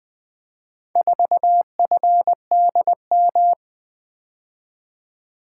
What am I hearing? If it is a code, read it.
Morse code, heard as a keyed tone, about 20 wpm, 700 Hz. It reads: 4FDM
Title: Four Letter-Number Combinations - 20wpm